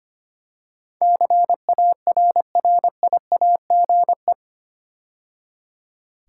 Morse code, heard as a keyed tone, about 25 wpm, 700 Hz. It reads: CARRIAGE
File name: 8 Letter Words - First 3 Letters Given - 25wpm